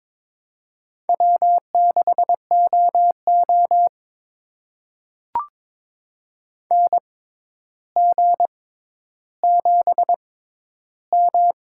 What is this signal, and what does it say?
Morse code, heard as a keyed tone, about 22 wpm, spaced as for 11 wpm, 700 Hz. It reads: W6OO NG7M
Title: USA Call Signs - 4 Characters - 6x Character Spacing - 22wpm